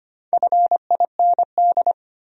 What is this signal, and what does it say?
Morse code, heard as a keyed tone, about 25 wpm, 700 Hz. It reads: FIND
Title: Top 100 Words 25wpm